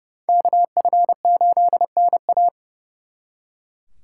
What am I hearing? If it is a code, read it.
Morse code, heard as a keyed tone, about 38 wpm, 700 Hz. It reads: KF8NA